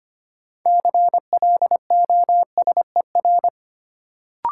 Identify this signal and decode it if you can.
Morse code, heard as a keyed tone, about 25 wpm, 700 Hz. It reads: CLOSER
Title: Sets of 5 Words Encore - New Word Drill - 25wpm